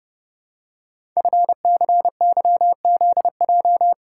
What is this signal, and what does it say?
Morse code, heard as a keyed tone, about 30 wpm, 700 Hz. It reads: FCYZJ
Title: Code Groups - 5 Letters - 30wpm